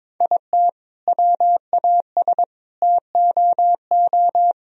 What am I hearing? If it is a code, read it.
Morse code, heard as a keyed tone, about 22 wpm, 700 Hz. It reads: IT WAS TOO
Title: Sentences from Top 1000 Words - Review Part 1 - 22wpm